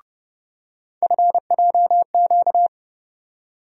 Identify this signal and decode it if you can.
Morse code, heard as a keyed tone, about 30 wpm, 700 Hz. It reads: FJQ
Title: USA Call Sign Postfixes - 3 Letters 30wpm